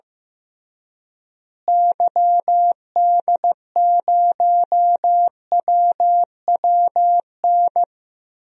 Morse code, heard as a keyed tone, about 15 wpm, 700 Hz. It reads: YD0WWN
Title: DXCC-100 Call Signs - 6 Characters - Repeated 3x - 15wpm